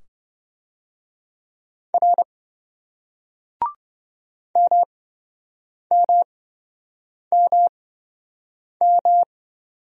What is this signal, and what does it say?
Morse code, heard as a keyed tone, about 27 wpm, spaced as for 7 wpm, 700 Hz. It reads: R MMMM